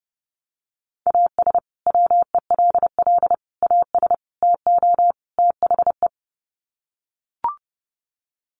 Morse code, heard as a keyed tone, about 30 wpm, 700 Hz. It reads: AS WELL AS TO THE
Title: Sets of 5 Words Encore2 30wpm